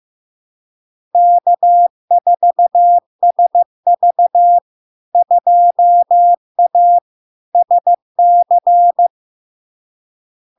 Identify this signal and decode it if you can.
Morse code, heard as a keyed tone, about 15 wpm, 700 Hz. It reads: K4SV 2A SC
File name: LARK-FD-2022-Log-in-CW-15wpm-700-Hz